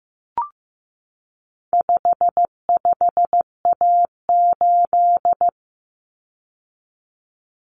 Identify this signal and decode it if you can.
Morse code, heard as a keyed tone, about 15 wpm, 700 Hz. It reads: 55A8
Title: Common Cut Numbers - 4-Digit 15wpm